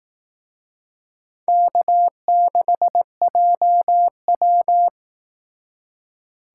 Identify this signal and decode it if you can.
Morse code, heard as a keyed tone, about 18 wpm, 700 Hz. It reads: K6JW
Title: Speed Racing - USA Call Signs 15wpm